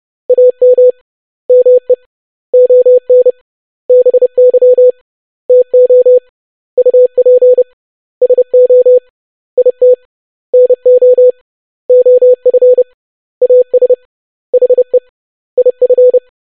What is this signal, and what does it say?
Morse code, heard as a keyed tone, about 30 wpm, 500 Hz. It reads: AM ME ON BY TO UP SO IT NO OF AS HE IF
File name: basic100-25wpm